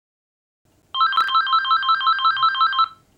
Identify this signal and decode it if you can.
There is no code.